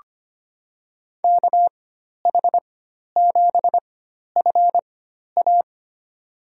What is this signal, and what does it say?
Morse code, heard as a keyed tone, about 25 wpm, spaced as for 12 wpm, 700 Hz. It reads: KH7FA
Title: USA Call Signs - 5 Characters - 4x Character Spacing - 25wpm